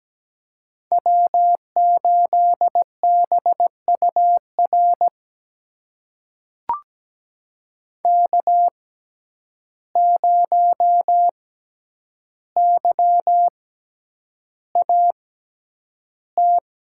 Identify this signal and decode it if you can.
Morse code, heard as a keyed tone, about 17 wpm, spaced as for 9 wpm, 700 Hz. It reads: W8BUR K0YAT
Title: USA Call Signs - 5 Characters - 6x Character Spacing - 17wpm